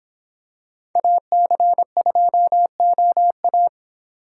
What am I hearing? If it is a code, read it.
Morse code, heard as a keyed tone, about 25 wpm, 700 Hz. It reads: AC2OA